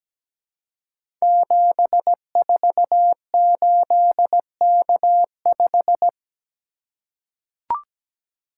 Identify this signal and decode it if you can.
Morse code, heard as a keyed tone, about 17 wpm, 700 Hz. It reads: 748K5
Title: Code Groups - 5 Letter:Numbers - 17wpm